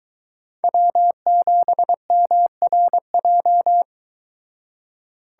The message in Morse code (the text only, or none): W7MRJ